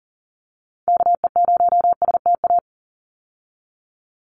Morse code, heard as a keyed tone, about 40 wpm, 700 Hz. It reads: KE0STA
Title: USA Call Signs 40wpm